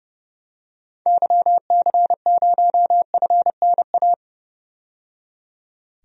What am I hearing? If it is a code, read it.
Morse code, heard as a keyed tone, about 30 wpm, 700 Hz. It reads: YC0FNA
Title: DXCC-100 Call Signs - 6 Characters - 30wpm